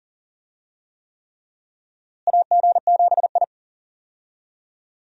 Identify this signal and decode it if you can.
Morse code, heard as a keyed tone, about 39 wpm, 700 Hz. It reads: AG7I
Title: USA Call Signs - 4 Characters - Repeated 3x - 40wpm